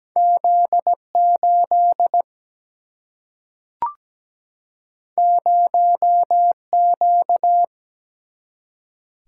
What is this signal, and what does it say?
Morse code, heard as a keyed tone, about 17 wpm, 700 Hz. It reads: Z8 0Q
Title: Single-Letter Single-Number 17wpm